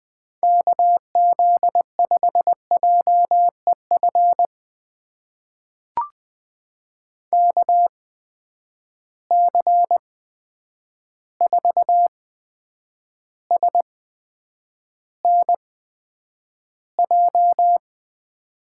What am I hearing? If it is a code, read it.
Morse code, heard as a keyed tone, about 20 wpm, spaced as for 9 wpm, 700 Hz. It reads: KZ5JEF KC4SNJ
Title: USA Call Signs - 6 Characters - 8x Character Spacing - 20wpm